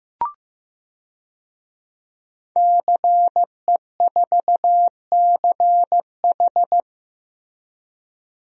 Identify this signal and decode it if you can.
Morse code, heard as a keyed tone, about 15 wpm, 700 Hz. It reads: CE4CH
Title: DXCC-100 Call Signs - 5 Characters - Repeated 2x - 15wpm